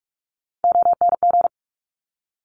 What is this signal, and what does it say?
Morse code, heard as a keyed tone, about 46 wpm, 700 Hz. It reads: ONG
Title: Top Three Letter Combinations in Words 45wpm